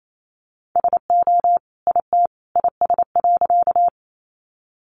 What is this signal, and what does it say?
Morse code, heard as a keyed tone, about 28 wpm, 700 Hz. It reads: SO IT IS.
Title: Sentences from Top 100 Words 28wpm